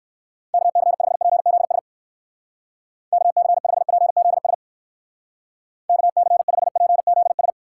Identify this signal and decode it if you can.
Morse code, heard as a keyed tone, about 62 wpm, 700 Hz. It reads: KC5PZS KC5PZS KC5PZS